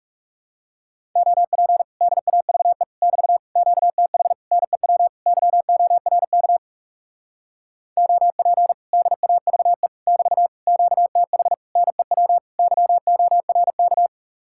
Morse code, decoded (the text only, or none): OP DAVE <BT> QTH NEW YORK OP DAVE <BT> QTH NEW YORK